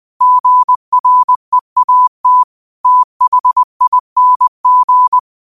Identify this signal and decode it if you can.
Morse code, heard as a keyed tone, about 20 wpm, 1000 Hz. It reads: GREAT THING
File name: qotdfiles / 20wpm